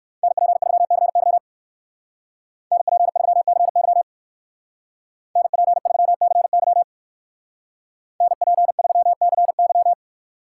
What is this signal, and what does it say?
Morse code, heard as a keyed tone, about 54 wpm, 700 Hz. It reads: NP3CY NP3CY NP3CY NP3CY
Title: Speed Racing - USA 2x2 Call Signs 45wpm